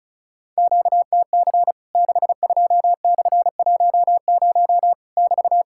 Morse code, heard as a keyed tone, about 34 wpm, 700 Hz. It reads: QTC 62/10 <BT>
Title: Worked All Europe DX Contest - QTC Exchange - 35wpm